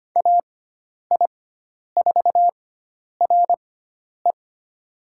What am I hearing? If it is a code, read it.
Morse code, heard as a keyed tone, about 25 wpm, spaced as for 10 wpm, 700 Hz. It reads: AI4RE
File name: Lesson 16g - Letter c - Call Signs 25wpm